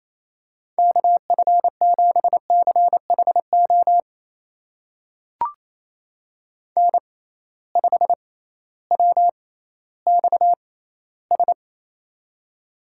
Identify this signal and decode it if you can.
Morse code, heard as a keyed tone, about 27 wpm, spaced as for 15 wpm, 700 Hz. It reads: KF7CHO N5WXS
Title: USA Call Signs - 6x Character Spacing - 28wpm